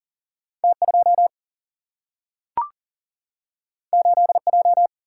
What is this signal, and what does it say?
Morse code, heard as a keyed tone, about 40 wpm, 700 Hz. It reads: T 2 8 J
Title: Single-Letter Single-Number 40wpm